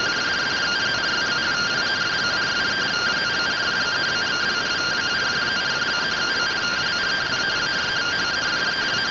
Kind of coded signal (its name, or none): two-tone teleprinter signal